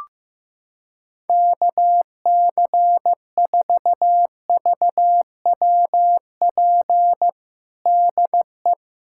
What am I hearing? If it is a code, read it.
Morse code, heard as a keyed tone, about 16 wpm, 700 Hz. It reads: KC4VWP DE
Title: USA Callsign DE Callsign 15wpm